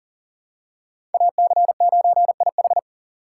Morse code, heard as a keyed tone, about 40 wpm, 700 Hz. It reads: AC9IH
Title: USA Call Signs - 5 Characters - 2x Character Spacing - 40wpm